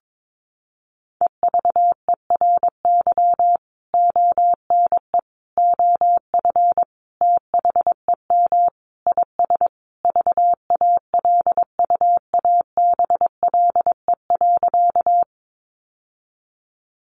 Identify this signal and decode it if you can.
Morse code, heard as a keyed tone, about 22 wpm, 700 Hz. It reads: EVERY ONE OF THEM IS VALUABLE.
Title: Sentences from Top 1100 Words 22wpm